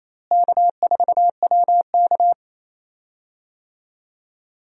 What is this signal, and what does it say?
Morse code, heard as a keyed tone, about 28 wpm, 700 Hz. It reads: K4WK
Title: USA Call Signs - 4 Characters - Repeated 2x - 28wpm